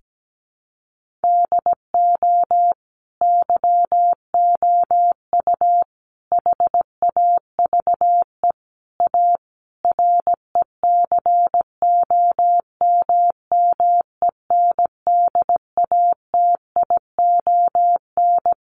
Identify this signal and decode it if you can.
Morse code, heard as a keyed tone, about 17 wpm, 700 Hz. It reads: DO YOU HAVE A RECOMMENDATION
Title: Sentences from Top 1400 Words 17wpm